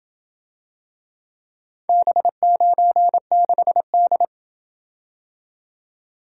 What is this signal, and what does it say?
Morse code, heard as a keyed tone, about 27 wpm, 700 Hz. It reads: B96D